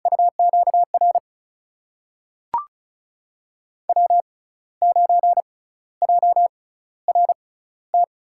Morse code, heard as a keyed tone, about 34 wpm, spaced as for 15 wpm, 700 Hz. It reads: UQR W9JRT
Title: USA Call Signs - 5 Characters - 6x Character Spacing - 35wpm